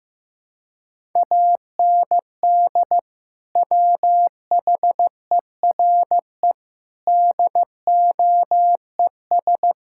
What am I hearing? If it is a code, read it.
Morse code, heard as a keyed tone, about 15 wpm, 700 Hz. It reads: AND WHERE DOES
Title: Sentences from Top 1000 Words - Review Part 5 - 15wpm